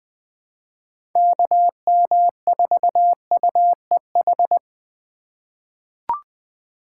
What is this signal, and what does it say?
Morse code, heard as a keyed tone, about 20 wpm, 700 Hz. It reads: KM4UEH
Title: USA Call Signs - 6 Characters - 2x Character Spacing - 20wpm